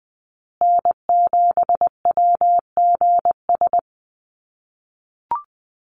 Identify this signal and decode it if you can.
Morse code, heard as a keyed tone, about 20 wpm, 700 Hz. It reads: N7WGS
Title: USA Call Signs Repeated 3x 20wpm